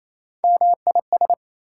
Morse code, heard as a keyed tone, about 40 wpm, spaced as for 34 wpm, 700 Hz. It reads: MIS
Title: Speed-Racing Prefixes and Suffixes 28wpm